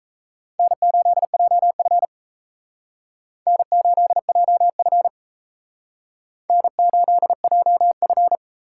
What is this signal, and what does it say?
Morse code, heard as a keyed tone, about 35 wpm, 700 Hz. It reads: N8JF N8JF N8JF
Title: Speed Racing - USA Call Signs 28wpm